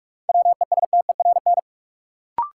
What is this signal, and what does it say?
Morse code, heard as a keyed tone, about 46 wpm, 700 Hz. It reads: WESTERN
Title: Sentences from Top 1000 Words - Review Part 7 - New Word Drill - 45wpm